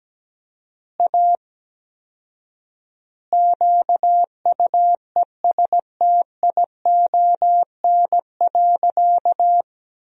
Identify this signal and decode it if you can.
Morse code, heard as a keyed tone, about 17 wpm, 700 Hz. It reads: A QUESTION.
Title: 4x Word-Spacing - Sentences from Top 400 Words - 17wpm